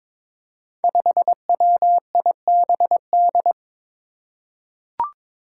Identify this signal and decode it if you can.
Morse code, heard as a keyed tone, about 22 wpm, 700 Hz. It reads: 5WIBD
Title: Five Challenging Letter-Number Combinations 22wpm